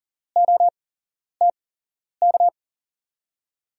Morse code, heard as a keyed tone, about 39 wpm, spaced as for 11 wpm, 700 Hz. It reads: OTK
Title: USA Call Signs - 6 Characters - 8x Character Spacing - 40wpm